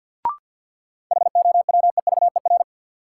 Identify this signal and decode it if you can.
Morse code, heard as a keyed tone, about 53 wpm, 700 Hz. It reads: HOWEVER
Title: Sentences from Top 600 Words - New Word Drill - 50wpm